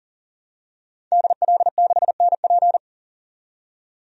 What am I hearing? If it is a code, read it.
Morse code, heard as a keyed tone, about 40 wpm, 700 Hz. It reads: DL6NP
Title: DXCC-100 Call Signs - 5 Characters - 40wpm